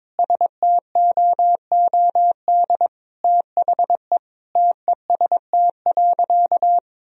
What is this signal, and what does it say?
Morse code, heard as a keyed tone, about 22 wpm, 700 Hz. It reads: STOOD THE TEST.
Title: Sentences from Top 1000 Words - Review Part 3 - 22wpm